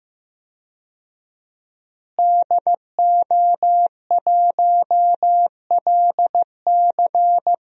Morse code, heard as a keyed tone, about 15 wpm, 700 Hz. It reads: DO1LC
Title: DXCC-100 Call Signs - 5 Characters - Repeated 2x - 15wpm